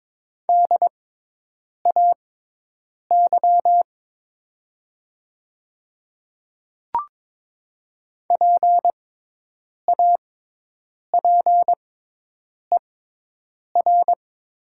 Morse code, heard as a keyed tone, about 22 wpm, spaced as for 8 wpm, 700 Hz. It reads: DAY PAPER
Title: 6x Character Spacing - Top 300 Words - 22wpm